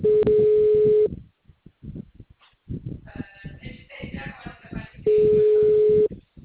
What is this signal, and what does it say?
Ringback tone (Europe, 425 Hz).